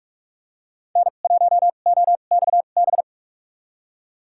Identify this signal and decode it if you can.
Morse code, heard as a keyed tone, about 47 wpm, spaced as for 33 wpm, 690 Hz. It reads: N1OXB
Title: USA Call Signs - 5 Characters - 2x Character Spacing - 45wpm